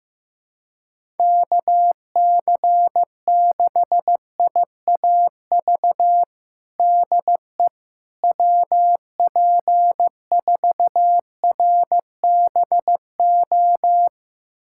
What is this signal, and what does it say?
Morse code, heard as a keyed tone, about 15 wpm, 700 Hz. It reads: KC6IAV DE WP4RBO
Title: USA Callsign DE Callsign 15wpm